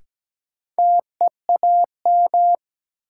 Morse code, heard as a keyed tone, about 17 wpm, 700 Hz. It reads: TEAM
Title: Sentences from Top 1500 Words - New Word Drill - 17wpm